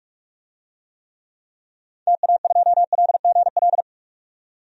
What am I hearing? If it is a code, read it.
Morse code, heard as a keyed tone, about 44 wpm, 700 Hz. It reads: TA2LGL
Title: DXCC-100 Call Signs - 6 Characters - 45wpm